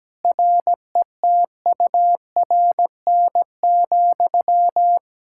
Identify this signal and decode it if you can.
Morse code, heard as a keyed tone, about 17 wpm, 700 Hz. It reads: RETURN,
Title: Sentences from Top 1000 Words - Review Part 5 - 17wpm